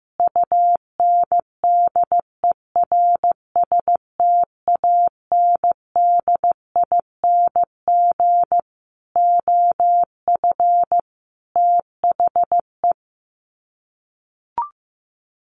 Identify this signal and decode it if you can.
Morse code, heard as a keyed tone, about 15 wpm, 700 Hz. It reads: UNDERSTANDING OF THE
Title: Sets of 5 Words Encore2 15wpm